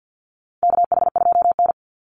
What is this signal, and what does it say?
Morse code, heard as a keyed tone, about 50 wpm, 700 Hz. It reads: KH2N
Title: Call Signs - 50wpm